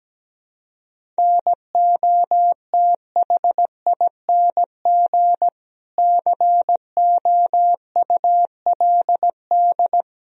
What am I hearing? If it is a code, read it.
Morse code, heard as a keyed tone, about 17 wpm, 700 Hz. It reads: NOTHING COULD
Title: Sentences from Top 1000 Words - Review Part 4 - 17wpm